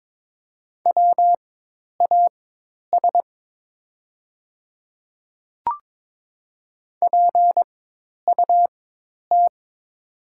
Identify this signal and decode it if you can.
Morse code, heard as a keyed tone, about 22 wpm, spaced as for 10 wpm, 700 Hz. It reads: WAS PUT
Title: 4x Character Spacing - Top 200 Words - 22wpm